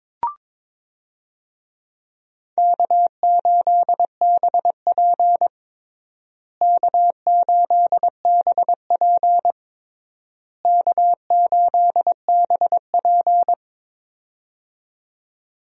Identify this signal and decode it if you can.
Morse code, heard as a keyed tone, about 22 wpm, 700 Hz. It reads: K8BP K8BP K8BP
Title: USA Call Signs - 4 Characters - Repeated 3x - 22wpm